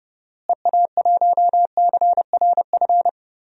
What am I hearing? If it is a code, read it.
Morse code, heard as a keyed tone, about 30 wpm, 700 Hz. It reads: EA1CRF
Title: DXCC-100 Call Signs - 6 Characters - Repeated 2x - 30wpm